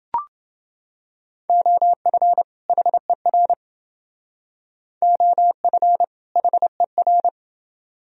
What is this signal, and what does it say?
Morse code, heard as a keyed tone, about 28 wpm, 700 Hz. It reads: OF HER OF HER